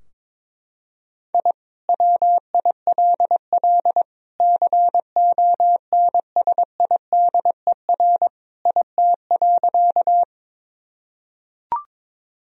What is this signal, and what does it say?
Morse code, heard as a keyed tone, about 21 wpm, 700 Hz. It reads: I WILL CONSIDER IT.
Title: Sentences from Top 1000 Words - Review Part 4 - 22wpm